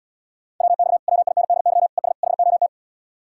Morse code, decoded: WX WINDY ES HOT

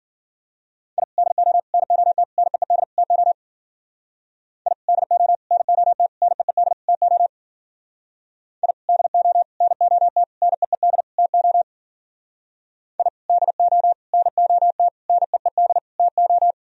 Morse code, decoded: I DO NOT NEED TO I DO NOT NEED TO I DO NOT NEED TO I DO NOT NEED TO

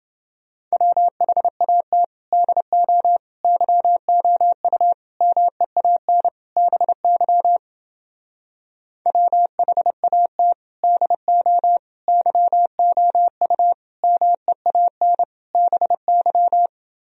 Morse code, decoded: WHAT DO YOU MEAN BY WHAT DO YOU MEAN BY

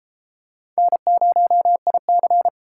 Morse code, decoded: N0IC

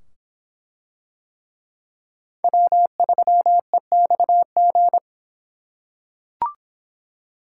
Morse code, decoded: W3EXG

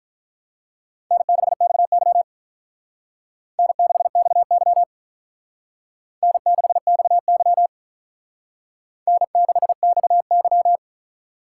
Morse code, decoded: N6XY N6XY N6XY N6XY

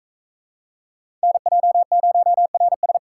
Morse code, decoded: NJ0RS

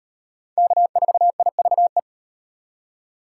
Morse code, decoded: K4IVE

